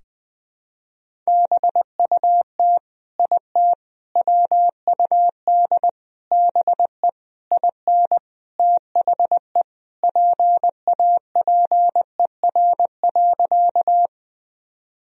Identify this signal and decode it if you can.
Morse code, heard as a keyed tone, about 20 wpm, 700 Hz. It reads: BUT IT WUD BE IN THE PAPER.
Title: Sentences from Top 300 Words and Abbreviations 20wpm